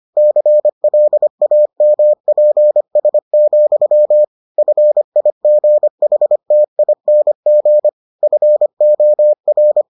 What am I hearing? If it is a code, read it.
Morse code, heard as a keyed tone, about 25 wpm, 600 Hz. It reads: CLAMPS, FIGHTING FOR